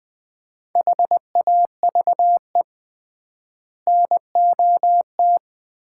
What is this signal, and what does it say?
Morse code, heard as a keyed tone, about 21 wpm, 700 Hz. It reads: HAVE NOT
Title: 3x Word-Spacing - Sets of 5 Words - 20wpm